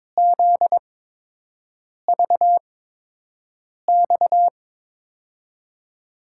Morse code, heard as a keyed tone, about 22 wpm, spaced as for 6 wpm, 700 Hz. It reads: ZVX
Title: USA Call Signs - 6 Characters - 8x Character Spacing - 22wpm